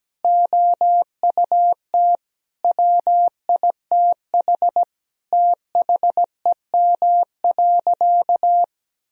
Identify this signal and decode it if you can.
Morse code, heard as a keyed tone, about 17 wpm, 700 Hz. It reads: OUT WITH THEM.
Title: Sentences from Top 1000 Words - Review Part 3 - 17wpm